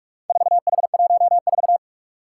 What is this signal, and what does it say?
Morse code, heard as a keyed tone, about 42 wpm, 700 Hz. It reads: 4H14
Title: Four Challenging Letter-Number Combinations 45wpm